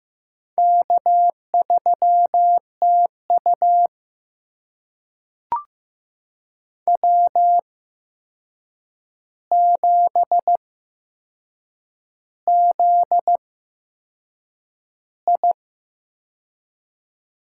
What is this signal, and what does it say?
Morse code, heard as a keyed tone, about 15 wpm, spaced as for 6 wpm, 700 Hz. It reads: K3TU W7ZI
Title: USA Call Signs - 4 Characters - 8x Character Spacing - 15wpm